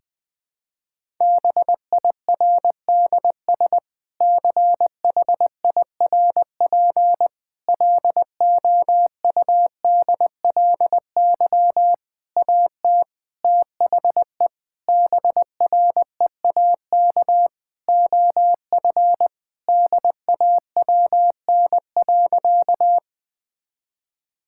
Morse code, heard as a keyed tone, about 21 wpm, 700 Hz. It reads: BIRDS CHIRP LOUDLY AT THE BREAK OF DAWN.
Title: Difficult Sentences with Hints - 20wpm